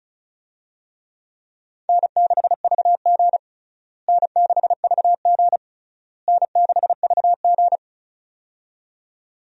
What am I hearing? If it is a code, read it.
Morse code, heard as a keyed tone, about 36 wpm, 700 Hz. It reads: N6VG N6VG N6VG
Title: USA Call Signs - 4 Characters - Repeated 3x - 35wpm